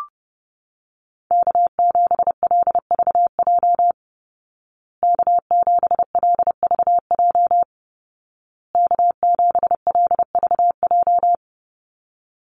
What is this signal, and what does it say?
Morse code, heard as a keyed tone, about 30 wpm, 700 Hz. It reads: K7LVJ K7LVJ K7LVJ
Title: Call Signs Repeated 2x - 30wpm